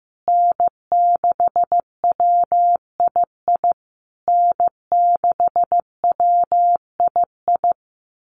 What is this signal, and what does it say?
Morse code, heard as a keyed tone, about 15 wpm, 700 Hz. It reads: N6WII N6WII
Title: USA Call Signs Repeated 3x 15wpm